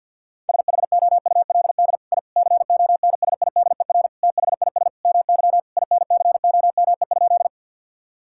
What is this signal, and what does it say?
Morse code, heard as a keyed tone, about 50 wpm, 700 Hz. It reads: SHOULD I CONSIDER THIS MY INCOME?
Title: Sentences from Top 1000 Words - Review Part 5 - 50wpm